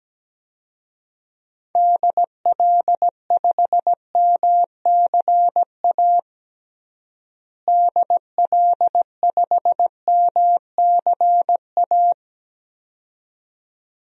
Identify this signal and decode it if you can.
Morse code, heard as a keyed tone, about 17 wpm, 700 Hz. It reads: DL5MCA DL5MCA
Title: DXCC-100 Call Signs - 6 Characters - Repeated 2x - 17wpm